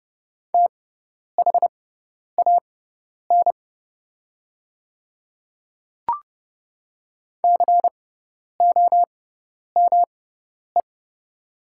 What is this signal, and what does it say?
Morse code, heard as a keyed tone, about 30 wpm, spaced as for 10 wpm, 700 Hz. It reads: THAN COME
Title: 6x Character Spacing - Top 100 Words - 30wpm